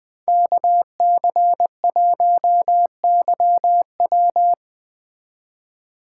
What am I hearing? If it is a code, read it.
Morse code, heard as a keyed tone, about 19 wpm, 700 Hz. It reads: KC1YW
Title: Speed Racing - USA Call Signs 17wpm